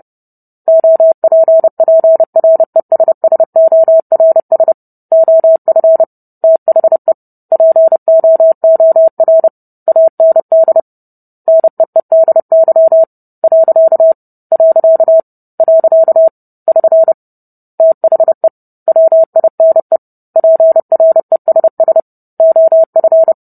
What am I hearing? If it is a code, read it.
Morse code, heard as a keyed tone, about 30 wpm, 650 Hz. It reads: OPPRESSORS OF THE POOR AND NEEDY . . <SN> THE WINE PRESS OF